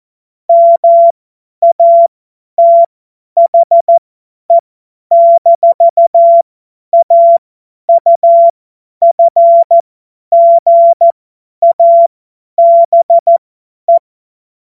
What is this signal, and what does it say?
Morse code, heard as a keyed tone, about 14 wpm, spaced as for 10 wpm, 680 Hz. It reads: MATHE-AUFGABE
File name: Morsekurs / Witze / Witz 14. März 2023 50 F70